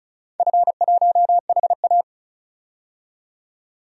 Morse code, heard as a keyed tone, about 36 wpm, 700 Hz. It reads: F1HA
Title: DXCC-100 Call Signs - 4 Characters - 35wpm